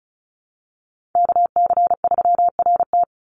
Morse code, heard as a keyed tone, about 34 wpm, 700 Hz. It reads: KC3RT
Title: NA SOTA Chaser Callsigns 35wpm